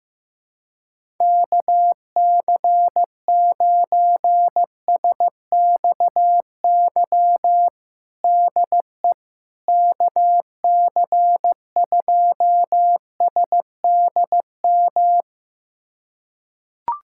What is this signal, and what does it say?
Morse code, heard as a keyed tone, about 16 wpm, 700 Hz. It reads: KC9SXY DE KC2SDM
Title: USA Callsign DE Callsign - 6 Characters - 15wpm